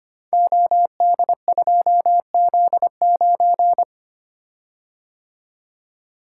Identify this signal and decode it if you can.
Morse code, heard as a keyed tone, about 25 wpm, 700 Hz. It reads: OD2Z9
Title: Code Groups - 5 Letter:Numbers - 25wpm